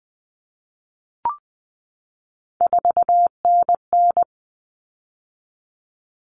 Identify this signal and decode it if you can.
Morse code, heard as a keyed tone, about 20 wpm, 700 Hz. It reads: 4NN
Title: Common Cut Numbers - 3-Digit 20wpm